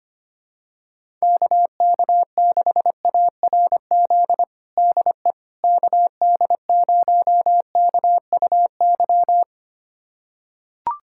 Morse code, heard as a keyed tone, about 25 wpm, 700 Hz. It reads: KK6ARZ DE KD0KUY
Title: USA Callsign DE Callsign 25wpm